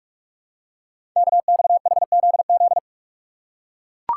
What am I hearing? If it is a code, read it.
Morse code, heard as a keyed tone, about 44 wpm, 700 Hz. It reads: KXHZZ
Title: Code Groups - 5 Letters - 45wpm